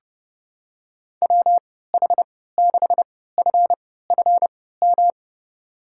Morse code, heard as a keyed tone, about 30 wpm, spaced as for 17 wpm, 700 Hz. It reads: WH6FFM
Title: USA Call Signs - 6 Characters - 3x Character Spacing - 30wpm